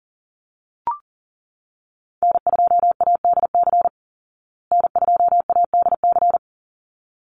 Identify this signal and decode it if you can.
Morse code, heard as a keyed tone, about 40 wpm, 700 Hz. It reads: N2ADC N2ADC
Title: Call Signs Repeated 3x - 40wpm